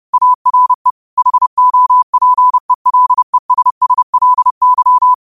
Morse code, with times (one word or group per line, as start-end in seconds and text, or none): 0.1-0.9 ARE
1.2-5.1 HOPELESSLY